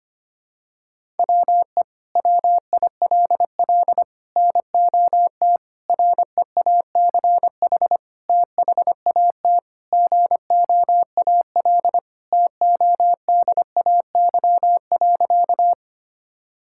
Morse code, heard as a keyed tone, about 25 wpm, 700 Hz. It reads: WE WILL NOT REACH THAT GOAL TODAY.
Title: Sentences from Top 1000 Words - Review Part 5 - 25wpm